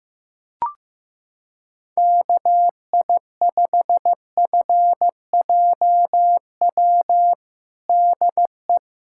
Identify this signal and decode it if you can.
Morse code, heard as a keyed tone, about 16 wpm, 700 Hz. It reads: KI5FJW DE
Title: USA Callsign DE Callsign - 6 Characters - 15wpm